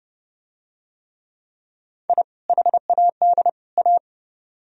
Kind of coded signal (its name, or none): Morse code